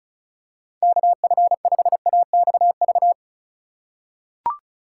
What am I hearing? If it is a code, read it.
Morse code, heard as a keyed tone, about 36 wpm, 700 Hz. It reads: KF5AXV
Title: USA Call Signs - 6 Characters - 4x Character Spacing - 35wpm